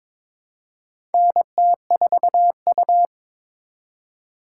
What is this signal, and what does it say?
Morse code, heard as a keyed tone, about 22 wpm, 700 Hz. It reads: NT4U
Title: USA Call Signs - 4 Characters - 6x Character Spacing - 22wpm